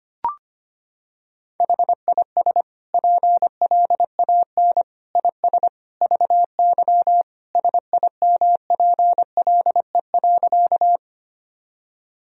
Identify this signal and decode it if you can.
Morse code, heard as a keyed tone, about 25 wpm, 700 Hz. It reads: HIS PLAN IS VY SIMPLE.
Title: Sentences from Top 500 Words and Abbreviations 25wpm